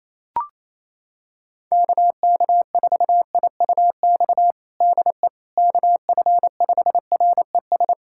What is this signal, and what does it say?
Morse code, heard as a keyed tone, about 28 wpm, 700 Hz. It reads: KK4IUX DE KF5RES
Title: USA Callsign DE Callsign 28wpm